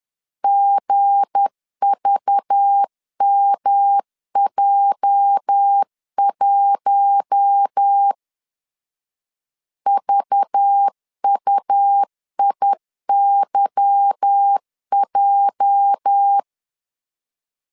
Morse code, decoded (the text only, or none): GVMJ1 VUIYJ